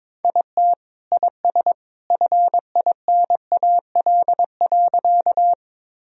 Morse code, heard as a keyed tone, about 22 wpm, 700 Hz. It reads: IT IS FINAL.
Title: Sentences from Top 1000 Words - Review Part 2 - 22wpm